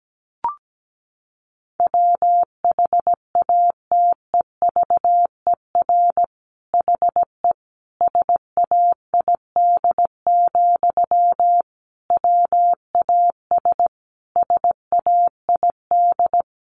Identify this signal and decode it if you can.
Morse code, heard as a keyed tone, about 17 wpm, 700 Hz. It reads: WHATEVER HE SAID, WAS SAID
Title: Sentences from Top 1100 Words 17wpm